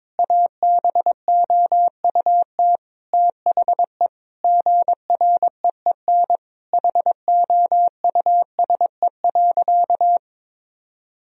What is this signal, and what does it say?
Morse code, heard as a keyed tone, about 22 wpm, 700 Hz. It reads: ABOUT THE GREEN HOUSE.
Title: Sentences from Top 1000 Words - Review Part 2 - 22wpm